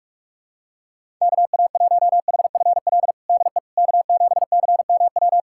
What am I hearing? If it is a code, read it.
Morse code, heard as a keyed tone, about 41 wpm, 700 Hz. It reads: KA1HFL DE K7CMW
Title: USA Callsign DE Callsign 45wpm